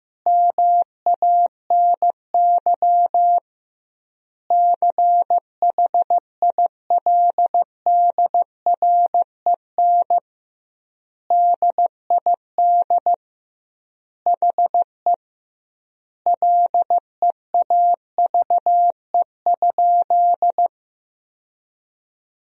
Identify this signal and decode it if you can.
Morse code, heard as a keyed tone, about 15 wpm, 700 Hz. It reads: MANY CHILDREN DID HE LEAVE?
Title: 2x Word-Spacing - Sentences from Top 400 Words - 15wpm